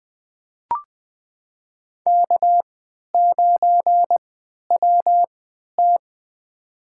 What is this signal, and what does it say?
Morse code, heard as a keyed tone, about 20 wpm, spaced as for 11 wpm, 700 Hz. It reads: K9WT
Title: USA Call Signs - 3x Character Spacing - 20wpm